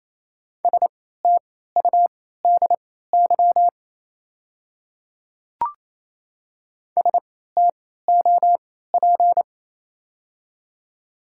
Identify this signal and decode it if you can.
Morse code, heard as a keyed tone, about 27 wpm, spaced as for 16 wpm, 700 Hz. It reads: STUDY STOP
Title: 3x Character Spacing - Top 300 Words - 28wpm